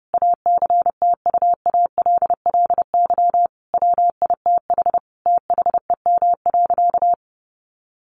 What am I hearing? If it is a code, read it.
Morse code, heard as a keyed tone, about 30 wpm, 700 Hz. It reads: ACTUALLY WITH THEM.
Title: Sentences from Top 700 Words 30wpm